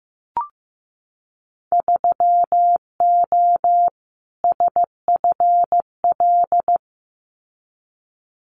Morse code, heard as a keyed tone, about 15 wpm, 700 Hz. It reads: 3O SFL
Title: Winter Field Day Exchanges 15wpm